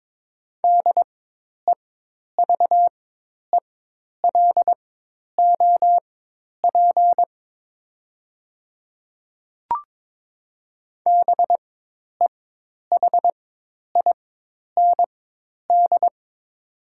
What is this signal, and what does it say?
Morse code, heard as a keyed tone, about 22 wpm, spaced as for 10 wpm, 700 Hz. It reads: DEVELOP BEHIND
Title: 4x Character Spacing - Top 500 Words - 22wpm